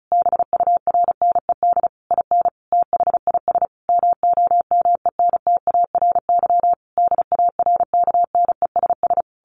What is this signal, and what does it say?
Morse code, heard as a keyed tone, about 34 wpm, 700 Hz. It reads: BURNED IN THIS MOMENTARY DARKNESS
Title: A Sound Of Thunder 35wpm